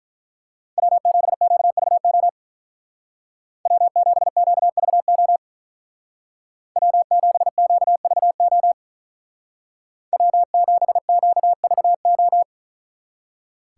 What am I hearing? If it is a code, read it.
Morse code, heard as a keyed tone, about 44 wpm, 700 Hz. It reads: W7QVO W7QVO W7QVO W7QVO